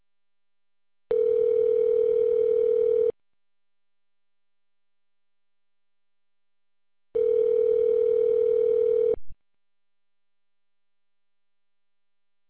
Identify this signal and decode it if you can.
Ringback tone (North America), 440 + 480 Hz.